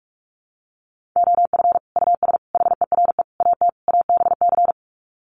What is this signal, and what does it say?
Morse code, heard as a keyed tone, about 44 wpm, 700 Hz. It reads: OF US HERE AT ABC
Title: Sets of 5 Words 45wpm